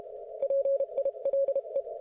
Morse code, heard as a keyed tone, about 32 wpm, spaced as for 28 wpm, 550 Hz. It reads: PILE